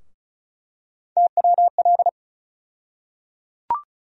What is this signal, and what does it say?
Morse code, heard as a keyed tone, about 36 wpm, 700 Hz. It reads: TWL
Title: USA Call Sign Postfixes - 3 Letters 35wpm